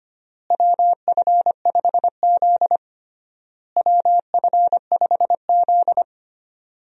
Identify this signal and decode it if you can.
Morse code, heard as a keyed tone, about 25 wpm, 700 Hz. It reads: WF5Z WF5Z
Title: USA Call Signs - 4 Characters - Repeated 3x - 25wpm